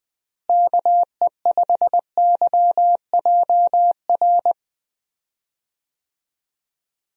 Morse code, heard as a keyed tone, about 21 wpm, 700 Hz. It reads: KE5YJR
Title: USA Call Signs - 6 Characters - Repeated 3x - 20wpm